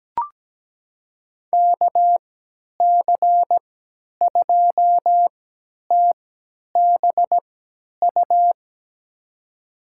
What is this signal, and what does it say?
Morse code, heard as a keyed tone, about 17 wpm, spaced as for 10 wpm, 700 Hz. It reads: KC2TBU
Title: USA Call Signs - 6 Characters - 3x Character Spacing - 17wpm